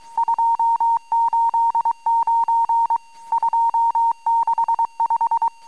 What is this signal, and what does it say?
Morse code, heard as a keyed tone, about 23 wpm, 910 Hz. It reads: 289 265
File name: amsat / cw